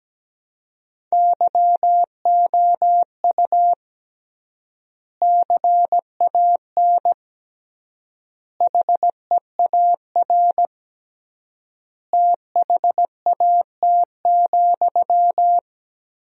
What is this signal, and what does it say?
Morse code, heard as a keyed tone, about 17 wpm, 700 Hz. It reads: YOU CAN HEAR THAT,
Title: 3x Word-Spacing - Sentences from Top 500 Words - 17wpm